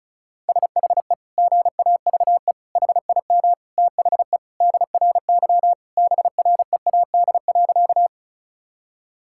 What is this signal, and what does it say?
Morse code, heard as a keyed tone, about 36 wpm, 700 Hz. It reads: SHE GAVE HIM THE DRY BREAD.
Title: Sentences from Top 1600 Words 35wpm